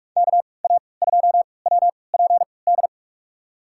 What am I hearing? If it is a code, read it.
Morse code, heard as a keyed tone, about 44 wpm, spaced as for 25 wpm, 700 Hz. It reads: KA2WPD
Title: USA Call Signs - 6 Characters - 3x Character Spacing - 45wpm